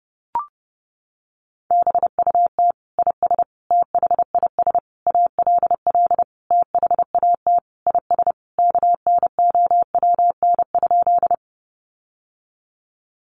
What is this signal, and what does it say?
Morse code, heard as a keyed tone, about 30 wpm, 700 Hz. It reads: BUT IS THIS ALL THAT IS KNOWN?
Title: Sentences from Top 600 Words 30wpm